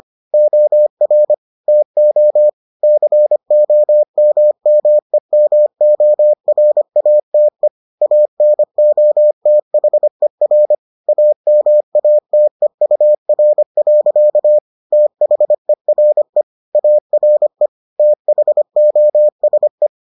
Morse code, heard as a keyed tone, about 25 wpm, 600 Hz. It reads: OR TO COMMEMORATE ANOTHER AMATEUR. THERE ARE THOSE